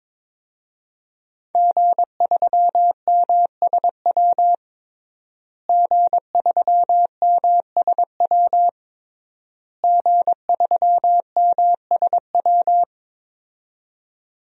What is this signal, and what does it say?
Morse code, heard as a keyed tone, about 22 wpm, 700 Hz. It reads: G3MSW G3MSW G3MSW
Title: DXCC-100 Call Signs - 5 Characters - Repeated 3x - 22wpm